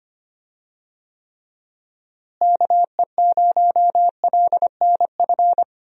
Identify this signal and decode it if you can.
Morse code, heard as a keyed tone, about 25 wpm, 700 Hz. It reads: KE0LNF
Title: USA Call Signs - 6 Characters - Repeated 3x - 25wpm